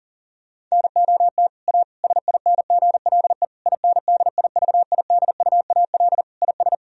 Morse code, heard as a keyed tone, about 40 wpm, 700 Hz. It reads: NOT A SINGLE INDIVIDUAL IS
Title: Sentences from Top 1000 Words - Review Part 4 - 40wpm